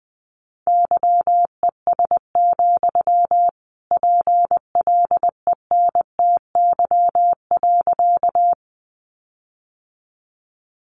Morse code, heard as a keyed tone, about 20 wpm, 700 Hz. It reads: YES, PLENTY.
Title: Sentences from Top 1300 Words 20wpm